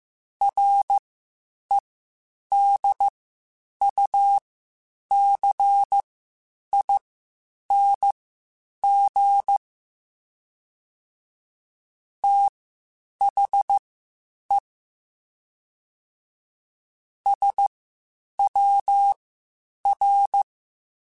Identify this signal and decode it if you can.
Morse code, heard as a keyed tone, about 15 wpm, spaced as for 8 wpm, 800 Hz. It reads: REDUCING THE SWR
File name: QSOs / QSOs-mp3s / WD0FBT 1@13